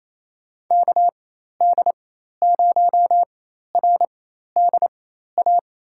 Morse code, heard as a keyed tone, about 28 wpm, spaced as for 13 wpm, 700 Hz. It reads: KD0RDA
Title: USA Call Signs - 4x Character Spacing - 28wpm